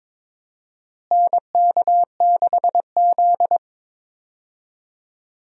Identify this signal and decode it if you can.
Morse code, heard as a keyed tone, about 23 wpm, 700 Hz. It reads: NK6Z
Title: USA Call Signs - 4 Characters - Repeated 3x - 22wpm